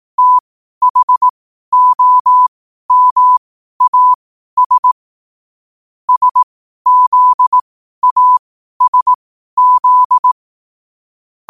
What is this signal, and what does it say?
Morse code, heard as a keyed tone, about 18 wpm, spaced as for 12 wpm, 1000 Hz. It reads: THOMAS SZASZ
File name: qotdfiles / 12wpm